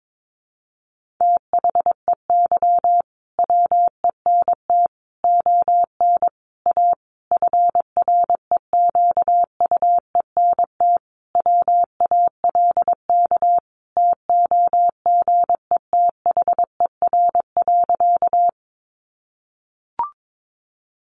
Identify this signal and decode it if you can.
Morse code, heard as a keyed tone, about 22 wpm, 700 Hz. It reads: THEY WENT ON A FREQUENT WALK TOGETHER.
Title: Sentences from Top 1400 Words 22wpm